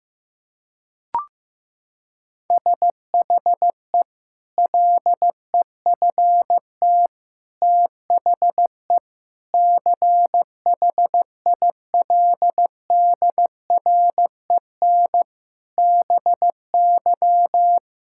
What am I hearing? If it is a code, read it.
Morse code, heard as a keyed tone, about 15 wpm, 700 Hz. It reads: SHE LEFT THE CHILDREN BY
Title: Sentences from Top 1000 Words - Review Part 5 - 15wpm